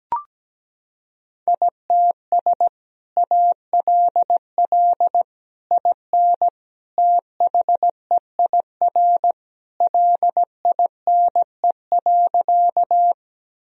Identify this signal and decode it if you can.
Morse code, heard as a keyed tone, about 17 wpm, 700 Hz. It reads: ITS ALL IN THEIR LINE.
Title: Sentences from Top 1000 Words - Review Part 1 - 17wpm